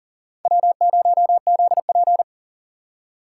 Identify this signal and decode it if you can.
Morse code, heard as a keyed tone, about 40 wpm, 700 Hz. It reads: W0ZP
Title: K1USN SST QSO Practice - 40wpm